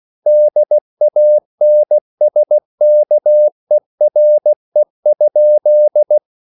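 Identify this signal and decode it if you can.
Morse code, heard as a keyed tone, about 16 wpm, 600 Hz. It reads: DANSKERE?
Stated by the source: Morse code at a slow 15 wpm from CW med Gnister.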